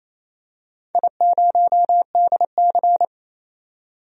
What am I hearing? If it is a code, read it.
Morse code, heard as a keyed tone, about 28 wpm, 700 Hz. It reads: I0DC
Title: DXCC-100 Call Signs - 4 Characters - 28wpm